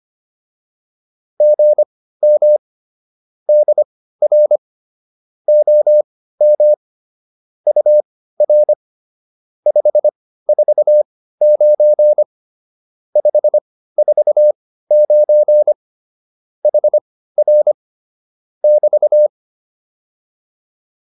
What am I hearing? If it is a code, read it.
Morse code, heard as a keyed tone, about 25 wpm, spaced as for 15 wpm, 600 Hz. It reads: GM DR OM UR 549 549 HR <BT>